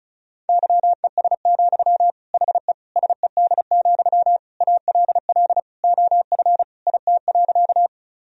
Morse code, heard as a keyed tone, about 36 wpm, 700 Hz. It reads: YES, HE SED, ALL OF IT.
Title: Sentences from Top 500 Words and Abbreviations 35wpm